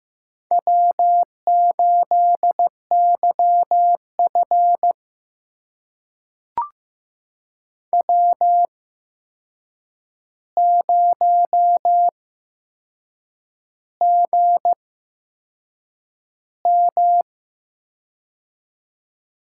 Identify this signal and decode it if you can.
Morse code, heard as a keyed tone, about 15 wpm, spaced as for 6 wpm, 700 Hz. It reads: W8YF W0GM
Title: USA Call Signs - 4 Characters - 8x Character Spacing - 15wpm